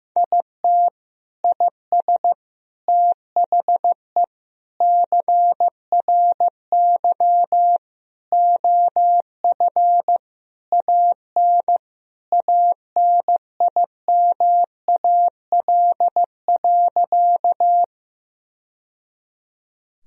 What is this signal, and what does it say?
Morse code, heard as a keyed tone, about 15 wpm, 700 Hz. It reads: IT IS THE CRY OF AN ANIMAL.
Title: Sentences from Top 1000 Words - Review Part 2 - 15wpm